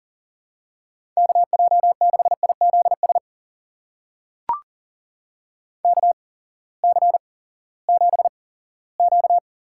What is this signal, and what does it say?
Morse code, heard as a keyed tone, about 38 wpm, spaced as for 20 wpm, 700 Hz. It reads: KJ6IZS KC7Q
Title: USA Call Signs - 8x Character Spacing - 40wpm